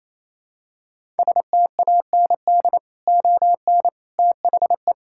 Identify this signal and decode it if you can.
Morse code, heard as a keyed tone, about 27 wpm, 700 Hz. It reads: STAND ON THE
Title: Sentences from Top 1000 Words - Review Part 5 - 28wpm